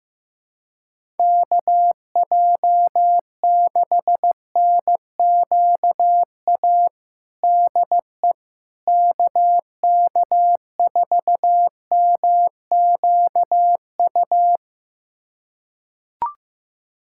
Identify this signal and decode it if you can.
Morse code, heard as a keyed tone, about 16 wpm, 700 Hz. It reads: KJ6NQA DE KK4MQU
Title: USA Callsign DE Callsign 15wpm